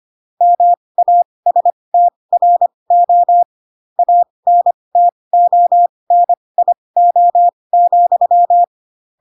Morse code, heard as a keyed tone, about 25 wpm, spaced as for 20 wpm, 700 Hz. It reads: MASTRO ANTONIO,
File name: Pinocchio / Pinocchio ch2 15 25 ik1hge